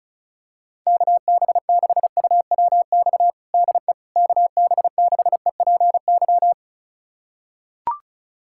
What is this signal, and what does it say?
Morse code, heard as a keyed tone, about 34 wpm, 690 Hz. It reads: KB6UWX DE KB6EPY
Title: USA Callsign DE Callsign 35wpm